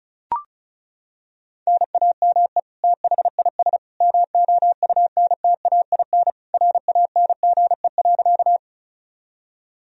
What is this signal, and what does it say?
Morse code, heard as a keyed tone, about 34 wpm, 700 Hz. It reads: NAME THIS MOUNTAIN RANGE.
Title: Sentences from Top 1000 Words - Review Part 4 - 35wpm